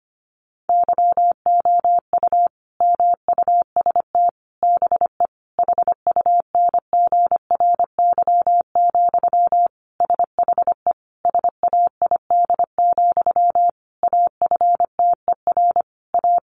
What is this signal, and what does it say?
Morse code, heard as a keyed tone, about 25 wpm, 700 Hz. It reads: YOU MUST BE HUNGRY, SHE SAID, AFTER A
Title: Sentences from Top 1400 Words 25wpm